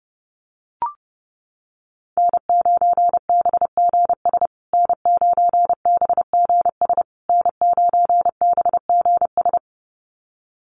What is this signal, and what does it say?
Morse code, heard as a keyed tone, about 30 wpm, 700 Hz. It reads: N9BGS N9BGS N9BGS
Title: USA Call Signs Repeated 3x 30wpm